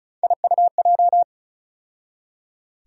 Morse code, heard as a keyed tone, about 34 wpm, 700 Hz. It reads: IUJ
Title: USA Call Sign Postfixes - 3 Letters 35wpm